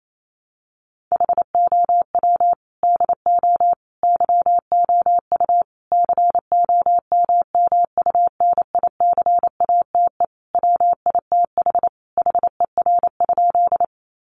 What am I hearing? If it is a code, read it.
Morse code, heard as a keyed tone, about 28 wpm, 700 Hz. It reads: HOW DO YOU COMMUNICATE WITH HER?
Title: Sentences from Top 1300 Words 28wpm